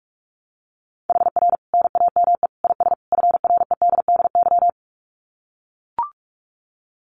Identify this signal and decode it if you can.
Morse code, heard as a keyed tone, about 44 wpm, 700 Hz. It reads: HR NAME IS FREDDY
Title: Common QSO Phrases 45wpm